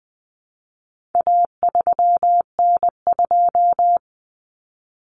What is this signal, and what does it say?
Morse code, heard as a keyed tone, about 20 wpm, 700 Hz. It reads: A3N2
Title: Common Cut Numbers - 4-Digit 20wpm